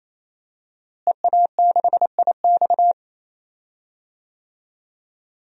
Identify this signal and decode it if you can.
Morse code, heard as a keyed tone, about 28 wpm, 700 Hz. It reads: EA6IX
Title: DXCC-100 Call Signs - 5 Characters - Repeated 3x - 28wpm